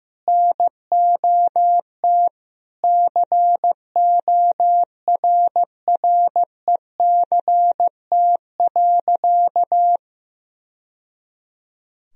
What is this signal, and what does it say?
Morse code, heard as a keyed tone, about 15 wpm, 700 Hz. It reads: NOT CORRECT.
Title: Sentences from Top 1000 Words - Review Part 2 - 15wpm